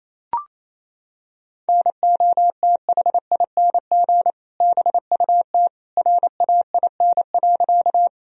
Morse code, heard as a keyed tone, about 28 wpm, 700 Hz. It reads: NOTHING BUT RAIN.
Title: Sentences from Top 1000 Words - Review Part 1 - 28wpm